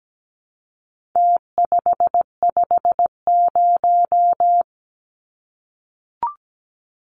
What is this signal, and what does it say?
Morse code, heard as a keyed tone, about 17 wpm, 700 Hz. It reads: T550
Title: Numbers - Contest Style - 4-Digit 17wpm